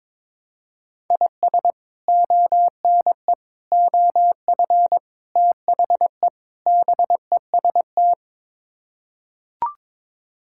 Morse code, heard as a keyed tone, about 22 wpm, 700 Hz. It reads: IS ONE OF THE BEST
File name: practice / 22wpm / Build A Phrase - 5-Word Phrases - 22wpm